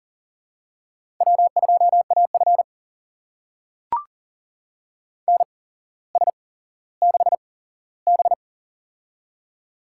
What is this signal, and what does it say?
Morse code, heard as a keyed tone, about 38 wpm, spaced as for 17 wpm, 700 Hz. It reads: W2AF NS6B
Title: USA Call Signs - 4 Characters - 8x Character Spacing - 40wpm